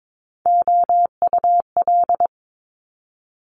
Morse code, heard as a keyed tone, about 22 wpm, 700 Hz. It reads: OUL
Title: Top Three Letter Combinations in Words 22wpm